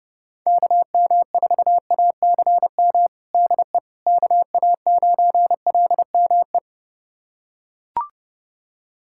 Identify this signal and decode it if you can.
Morse code, heard as a keyed tone, about 32 wpm, 700 Hz. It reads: KM4ACM DE KA9LME
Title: USA Callsign DE Callsign 30wpm